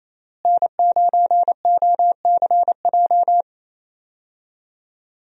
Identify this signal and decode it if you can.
Morse code, heard as a keyed tone, about 30 wpm, 700 Hz. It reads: N9OCJ
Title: USA Call Signs - 5 Characters - Repeated 3x - 28wpm